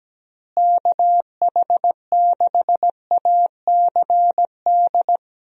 Morse code, heard as a keyed tone, about 17 wpm, 700 Hz. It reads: KH6ACD
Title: USA Call Signs - 6 Characters - Repeated 3x - 17wpm